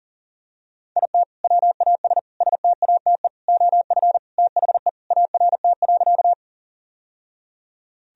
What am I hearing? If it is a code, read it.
Morse code, heard as a keyed tone, about 40 wpm, 700 Hz. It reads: IT WAS STATE OF THE ART.
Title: Sentences from Top 1000 Words - Review Part 3 - 40wpm